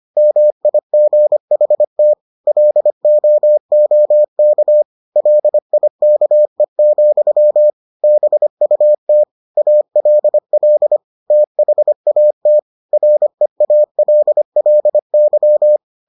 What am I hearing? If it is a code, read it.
Morse code, heard as a keyed tone, about 25 wpm, 600 Hz. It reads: MIGHT LOOK LIKE, BUT ALL THAT REALLY